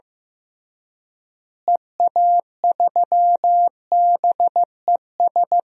Morse code, heard as a keyed tone, about 15 wpm, 700 Hz. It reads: EA3BES
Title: DXCC-100 Call Signs - 6 Characters - Repeated 3x - 15wpm